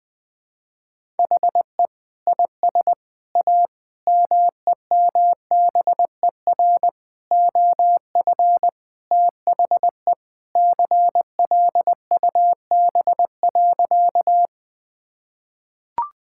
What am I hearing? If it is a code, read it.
Morse code, heard as a keyed tone, about 20 wpm, 700 Hz. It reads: HE IS A MEMBER OF THE CLUB.
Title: Sentences with 3-Word Phrases - 20wpm